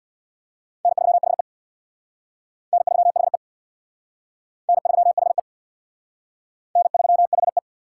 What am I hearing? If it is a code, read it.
Morse code, heard as a keyed tone, about 60 wpm, 700 Hz. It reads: N3HE N3HE N3HE N3HE
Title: Speed Racing - USA 1x2 Call Signs 50wpm